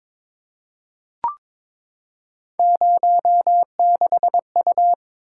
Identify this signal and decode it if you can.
Morse code, heard as a keyed tone, about 22 wpm, 700 Hz. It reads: 06U
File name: practice / 22wpm / Three Letter-Number Combinations - 22wpm